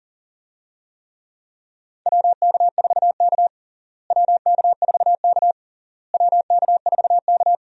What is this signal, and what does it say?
Morse code, heard as a keyed tone, about 40 wpm, 700 Hz. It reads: WK4K WK4K WK4K
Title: USA Call Signs - 4 Characters - Repeated 3x - 40wpm